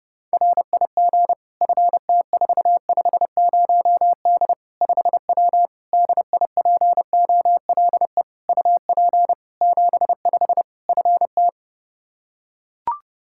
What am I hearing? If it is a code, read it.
Morse code, heard as a keyed tone, about 32 wpm, 700 Hz. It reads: RIG FT450D 5W DIPOLE UP 75 FT
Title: Practice QSO - Exchange 2 - Info - 30wpm